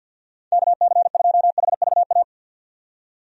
Morse code, heard as a keyed tone, about 50 wpm, 700 Hz. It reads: KC2HVA
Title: USA Call Signs - 6 Characters - 50wpm